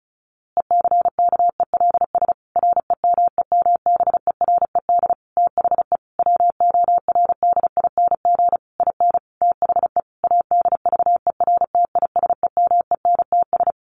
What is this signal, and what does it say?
Morse code, heard as a keyed tone, about 34 wpm, 700 Hz. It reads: ECKELS REMEMBERED THE WORDING IN THE ADVERTISEMENTS
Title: A Sound Of Thunder 35wpm